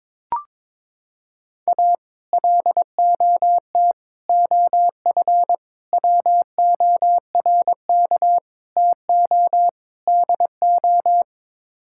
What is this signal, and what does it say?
Morse code, heard as a keyed tone, about 22 wpm, 700 Hz. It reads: A LOT OF WORK TO DO
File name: practice / 22wpm / Sets of 6 Words - 22wpm